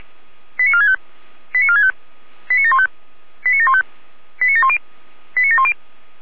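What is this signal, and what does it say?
Five-tone selective call (ZVEI), heard as quick runs of five tones, tones of about 70 ms. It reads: 89465 89465 89715 89715 89710 89710
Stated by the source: ZVEI five-tone call received by SDR.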